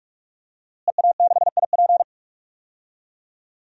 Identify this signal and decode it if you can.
Morse code, heard as a keyed tone, about 44 wpm, 700 Hz. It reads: EA6IP
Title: DXCC-100 Call Signs - 5 Characters - Repeated 3x - 45wpm